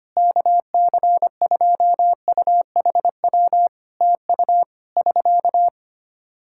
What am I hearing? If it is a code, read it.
Morse code, heard as a keyed tone, about 25 wpm, 700 Hz. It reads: KC2UHW TU <SK>
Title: Practice QSO - Exchange 4 - Ending - 25wpm